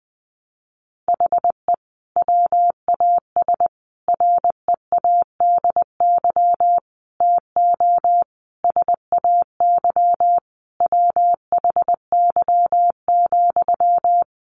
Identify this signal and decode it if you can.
Morse code, heard as a keyed tone, about 20 wpm, 700 Hz. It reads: HE WAS READY TO SAY WHY,
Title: Sentences from Top 1300 Words 20wpm